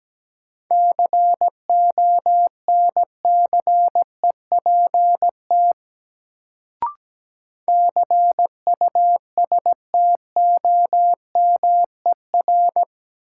Morse code, heard as a keyed tone, about 17 wpm, 700 Hz. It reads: CONCEPT CUSTOMER
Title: Sentences from Top 1100 Words - New Word Drill - 17wpm